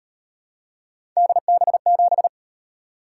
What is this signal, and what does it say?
Morse code, heard as a keyed tone, about 37 wpm, 700 Hz. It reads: DB7